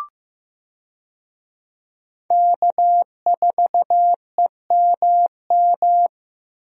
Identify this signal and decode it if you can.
Morse code, heard as a keyed tone, about 15 wpm, 700 Hz. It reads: K4EMM
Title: USA Call Signs - 5 Characters - Repeated 2x - 15wpm